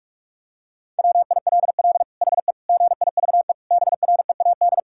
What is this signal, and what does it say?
Morse code, heard as a keyed tone, about 46 wpm, 700 Hz. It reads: WILL HE GIVE BREAD
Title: Sentences from Top 1600 Words 45wpm